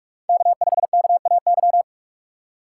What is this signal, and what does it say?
Morse code, heard as a keyed tone, about 46 wpm, 700 Hz. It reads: K5KAY
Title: USA Call Signs - 5 Characters - 2x Character Spacing - 45wpm